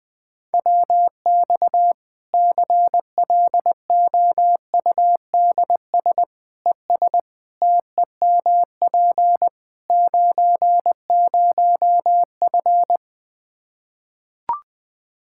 Morse code, heard as a keyed tone, about 20 wpm, 700 Hz. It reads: WX CLOUDS ES TEMP 90F
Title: Practice QSO - Exchange 2 - Info - 20wpm